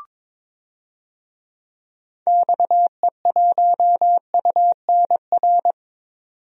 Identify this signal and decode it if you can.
Morse code, heard as a keyed tone, about 22 wpm, 700 Hz. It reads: XE1UNR
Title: DXCC-100 Call Signs - 6 Characters - 22wpm